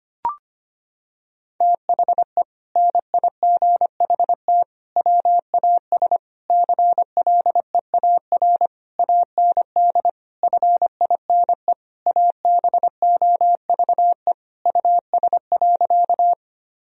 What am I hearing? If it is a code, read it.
Morse code, heard as a keyed tone, about 25 wpm, 700 Hz. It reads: THE NIGHT WAS CLEAR AND FINE ABOVE US.
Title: Sentences from Top 1000 Words - Review Part 2 - 25wpm